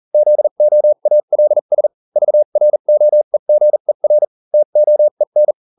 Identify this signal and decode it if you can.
Morse code, heard as a keyed tone, about 40 wpm, spaced as for 35 wpm, 600 Hz. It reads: ZOALS VROEGER TOEN